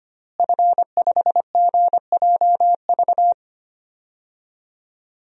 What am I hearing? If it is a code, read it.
Morse code, heard as a keyed tone, about 25 wpm, 700 Hz. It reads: F5GJV
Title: DXCC-100 Call Signs - 5 Characters - Repeated 3x - 25wpm